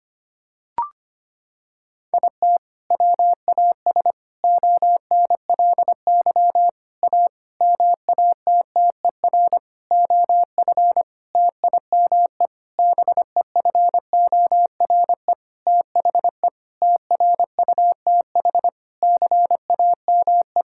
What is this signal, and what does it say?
Morse code, heard as a keyed tone, about 25 wpm, 700 Hz. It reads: IT WAS ONLY A MATTER OF TIME BEFORE THE TRUTH CAME
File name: practice / 25wpm / Sentences with 9-Word Phrases - 25wpm